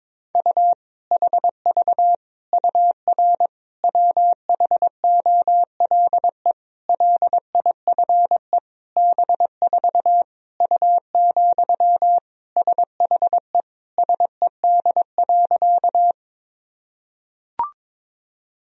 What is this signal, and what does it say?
Morse code, heard as a keyed tone, about 22 wpm, 700 Hz. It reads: U HV UR WHOLE LIFE B4 U, SHE SED.
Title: Sentences from Top 400 Words and Abbreviations 22wpm